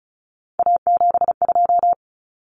Morse code, heard as a keyed tone, about 34 wpm, 700 Hz. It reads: A72
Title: Common Cut Numbers - 3-Digit 35wpm